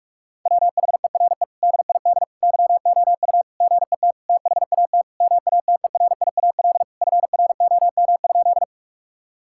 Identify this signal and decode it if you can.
Morse code, heard as a keyed tone, about 46 wpm, 700 Hz. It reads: WHERE DID YOU GET THAT MATERIAL FROM?
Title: Sentences from Top 1000 Words - Review Part 5 - 45wpm